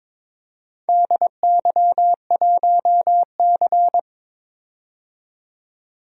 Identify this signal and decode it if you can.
Morse code, heard as a keyed tone, about 22 wpm, 700 Hz. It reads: DY1C
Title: DXCC-100 Call Signs - 4 Characters - 22wpm